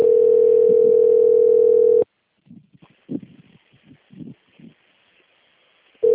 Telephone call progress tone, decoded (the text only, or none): ringback tone